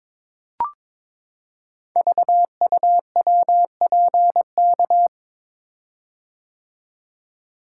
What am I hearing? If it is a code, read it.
Morse code, heard as a keyed tone, about 22 wpm, 700 Hz. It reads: VUWPK
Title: Code Groups - 5 Letters - 22wpm